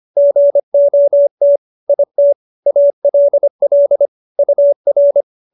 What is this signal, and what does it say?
Morse code, heard as a keyed tone, about 25 wpm, 570 Hz. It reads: GOT IT ALL UR
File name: QSO302 25